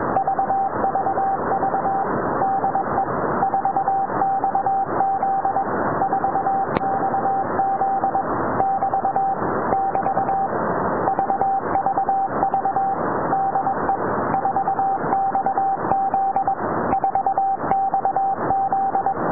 Morse code, heard as a keyed tone, about 22 wpm, 780 Hz, with radio noise. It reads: VVV DE 4XZ 4XZ <BT> <BT> VVV DE 4XZ 4XZ